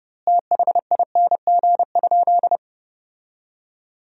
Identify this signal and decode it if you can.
Morse code, heard as a keyed tone, about 30 wpm, 700 Hz. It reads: THING?
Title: 2x Word-Spacing - Sentences from Top 200 Words - 30wpm